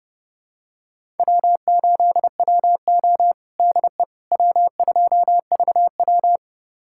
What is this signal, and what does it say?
Morse code, heard as a keyed tone, about 30 wpm, 700 Hz. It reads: W8WO DE W2VW
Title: USA Callsign DE Callsign - 4 Characters - 30wpm